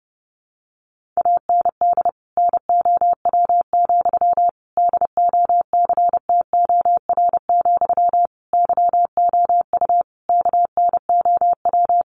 Morse code, heard as a keyed tone, about 30 wpm, 700 Hz. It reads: AND NOW, DOCTOR, YOU KNOW
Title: Sentences from Top 1500 Words 30wpm